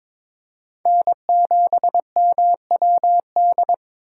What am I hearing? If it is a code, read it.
Morse code, heard as a keyed tone, about 22 wpm, 700 Hz. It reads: N7MWD
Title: USA Call Signs - 5 Characters - 3x Character Spacing - 22wpm